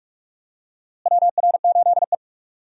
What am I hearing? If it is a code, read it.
Morse code, heard as a keyed tone, about 47 wpm, 700 Hz. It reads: WR8E
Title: USA Call Signs - 4 Characters - Repeated 3x - 45wpm